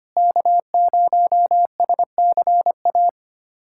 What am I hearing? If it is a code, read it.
Morse code, heard as a keyed tone, about 25 wpm, 700 Hz. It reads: K0SCA
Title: USA Callsign DE Callsign - 5 Characters - 25wpm